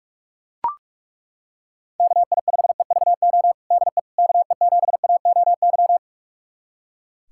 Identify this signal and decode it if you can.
Morse code, heard as a keyed tone, about 41 wpm, 690 Hz. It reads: KI5EVO DE KE7AOY
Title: USA Callsign DE Callsign 45wpm